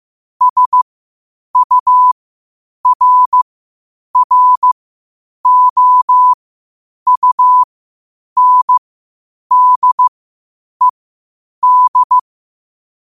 Morse code, heard as a keyed tone, about 15 wpm, spaced as for 8 wpm, 1000 Hz. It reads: SURROUNDED
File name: qotdfiles / 8wpm